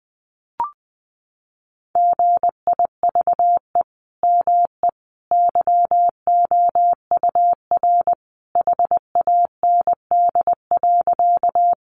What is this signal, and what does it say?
Morse code, heard as a keyed tone, about 19 wpm, 700 Hz. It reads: GIVE ME YOUR HAND.
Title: Sentences from Top 200 Words 20wpm